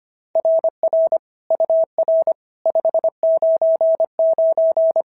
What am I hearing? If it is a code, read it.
Morse code, heard as a keyed tone, about 22 wpm, 650 Hz. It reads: RR UR 599
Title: SOTA QSO Practice 25wpm